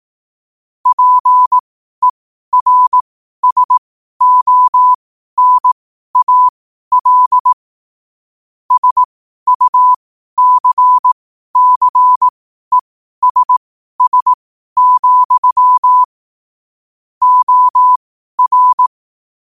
Morse code, heard as a keyed tone, about 18 wpm, spaced as for 12 wpm, 1000 Hz. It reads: PERSONAL SUCCESS, OR
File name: qotdfiles / 12wpm